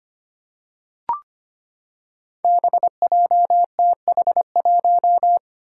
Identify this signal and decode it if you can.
Morse code, heard as a keyed tone, about 25 wpm, 700 Hz. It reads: BJTH1
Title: Five Challenging Letter-Number Combinations 25wpm